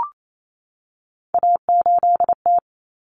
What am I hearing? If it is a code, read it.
Morse code, heard as a keyed tone, about 28 wpm, 700 Hz. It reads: A8T
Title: Common Cut Numbers - 3-Digit 28wpm